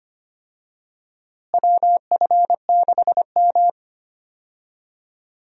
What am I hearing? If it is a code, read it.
Morse code, heard as a keyed tone, about 25 wpm, 700 Hz. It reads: WF6M
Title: USA Call Signs - 4 Characters - Repeated 3x - 25wpm